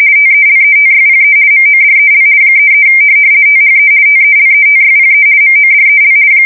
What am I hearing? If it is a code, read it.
RTTY, about 50 baud. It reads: TELETYPE MODEL 33 THE TELETYPE MODEL
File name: fsk170-50B / teletype-corp